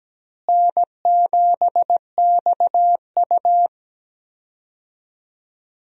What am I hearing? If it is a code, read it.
Morse code, heard as a keyed tone, about 17 wpm, 700 Hz. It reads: N7XU
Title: USA Call Signs - 4 Characters - Repeated 3x - 17wpm